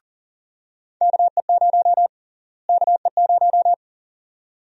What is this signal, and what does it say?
Morse code, heard as a keyed tone, about 40 wpm, 700 Hz. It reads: KE0 KE0
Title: USA Call Sign Prefixes - Repeated 2x - 40wpm